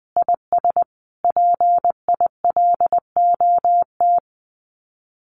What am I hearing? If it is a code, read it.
Morse code, heard as a keyed tone, about 20 wpm, 700 Hz. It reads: IS PILOT
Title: Common QSO Phrases Encore 20wpm